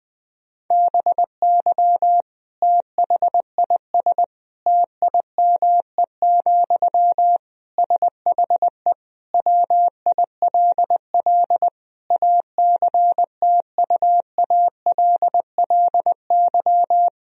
Morse code, heard as a keyed tone, about 20 wpm, 700 Hz. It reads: BY THIS TIME, SHE WILL ACTUALLY
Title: Sentences from Top 1000 Words - Review Part 4 - 20wpm